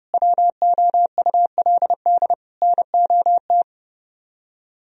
Morse code, heard as a keyed tone, about 31 wpm, 700 Hz. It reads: WOULD NOT